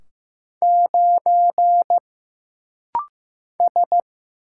Morse code, heard as a keyed tone, about 15 wpm, 700 Hz. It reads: 9 S
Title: Single Letter-Number Alt - Rapid-Fire 15wpm